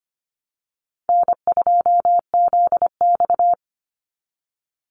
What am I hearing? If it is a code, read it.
Morse code, heard as a keyed tone, about 25 wpm, 700 Hz. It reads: N2ZX
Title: Call Signs Repeated 2x - 25wpm